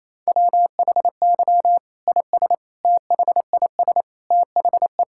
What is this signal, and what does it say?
Morse code, heard as a keyed tone, about 28 wpm, 700 Hz. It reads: WHY IS THIS THE
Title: Sentences from Top 1000 Words - Review Part 5 - 28wpm